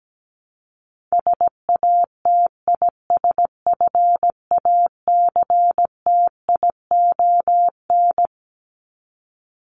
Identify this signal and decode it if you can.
Morse code, heard as a keyed tone, about 17 wpm, 700 Hz. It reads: SATISFACTION
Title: Top 1751-2000 Words 17wpm